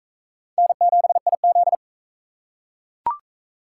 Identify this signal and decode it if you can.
Morse code, heard as a keyed tone, about 41 wpm, 700 Hz. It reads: N7IZ